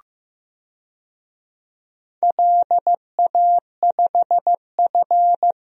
Morse code, heard as a keyed tone, about 15 wpm, 700 Hz. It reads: LA5F
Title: DXCC-100 Call Signs - 4 Characters - 15wpm